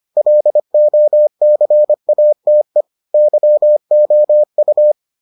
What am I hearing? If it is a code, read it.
Morse code, heard as a keyed tone, about 25 wpm, 600 Hz. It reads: LOCATE YOU